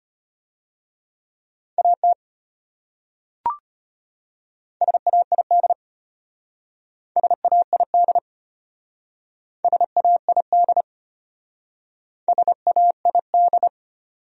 Morse code, heard as a keyed tone, about 33 wpm, 700 Hz. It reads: AT SAID SAID SAID SAID